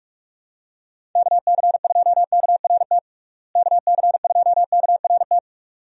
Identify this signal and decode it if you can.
Morse code, heard as a keyed tone, about 42 wpm, 700 Hz. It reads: KC2KRT KC2KRT
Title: USA Call Signs - 6 Characters - Repeated 2x - 45wpm